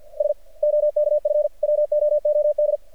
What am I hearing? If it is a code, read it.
Morse code, heard as a keyed tone, about 49 wpm, 600 Hz. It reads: S OKF WOOD
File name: archived / CRSnet